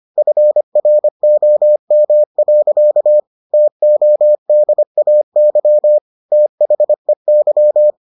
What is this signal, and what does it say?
Morse code, heard as a keyed tone, about 25 wpm, 600 Hz. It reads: FROM. TODAY THEY